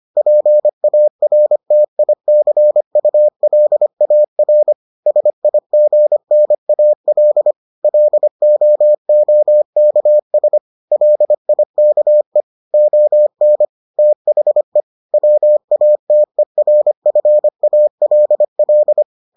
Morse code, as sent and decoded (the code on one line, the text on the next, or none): .--. .- .-. - .. -.-. ..- .-.. .- .-. / ... .. --. -. .- .-.. / .-.. --- --- -.- ... / .-.. .. -.- . / --- -. / - .... / .-- .- - . .-. ..-. .- .-.. .-..
PARTICULAR SIGNAL LOOKS LIKE ON THE WATERFALL